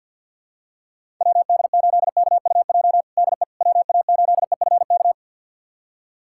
Morse code, heard as a keyed tone, about 48 wpm, 700 Hz. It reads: WD8KUJ DE WA8EFK
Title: USA Callsign DE Callsign 50wpm